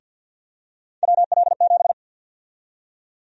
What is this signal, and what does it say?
Morse code, heard as a keyed tone, about 49 wpm, 700 Hz. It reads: WL7
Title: USA Call Sign Prefixes - Repeated 2x - 50wpm